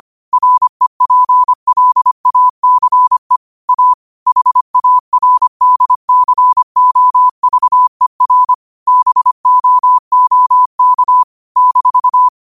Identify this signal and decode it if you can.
Morse code, heard as a keyed tone, about 25 wpm, 1000 Hz. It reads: REPLACE A HARDCOVER BOOK -
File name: qotdfiles / 25wpm